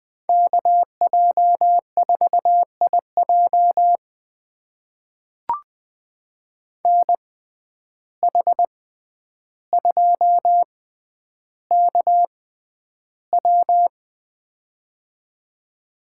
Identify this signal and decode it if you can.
Morse code, heard as a keyed tone, about 20 wpm, spaced as for 10 wpm, 700 Hz. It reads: KJ4IJ NH2KW
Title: USA Call Signs - 5 Characters - 6x Character Spacing - 20wpm